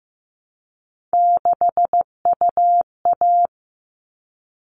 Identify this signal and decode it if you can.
Morse code, heard as a keyed tone, about 15 wpm, 700 Hz. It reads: 6UA
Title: All Cut Numbers - 3-Digit 15wpm